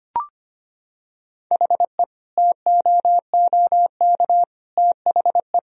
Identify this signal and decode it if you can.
Morse code, heard as a keyed tone, about 25 wpm, 700 Hz. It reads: HE TOOK THE
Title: Sentences from Top 1000 Words - Review Part 2 - 25wpm